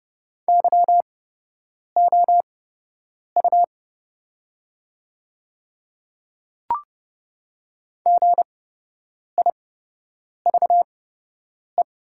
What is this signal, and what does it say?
Morse code, heard as a keyed tone, about 30 wpm, spaced as for 8 wpm, 700 Hz. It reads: YOU GIVE
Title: 8x Character Spacing - Top 300 Words - 30wpm